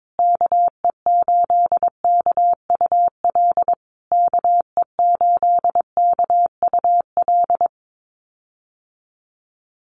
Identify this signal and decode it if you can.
Morse code, heard as a keyed tone, about 22 wpm, 700 Hz. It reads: KE8KUL KE8KUL
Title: USA Call Signs Repeated 3x 22wpm